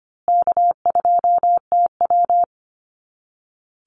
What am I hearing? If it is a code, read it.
Morse code, heard as a keyed tone, about 25 wpm, 700 Hz. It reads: K2TW
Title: Easy Call Signs Repeated 3x - 25wpm